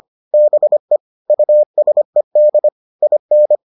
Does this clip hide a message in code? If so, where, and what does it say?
Yes: Morse code from 0.3 to 3.6 s, heard as a keyed tone, about 25 wpm, 600 Hz. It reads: BE USED IN